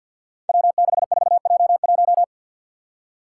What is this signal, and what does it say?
Morse code, heard as a keyed tone, about 52 wpm, 700 Hz. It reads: W64J1
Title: Five Challenging Letter-Number Combinations 50wpm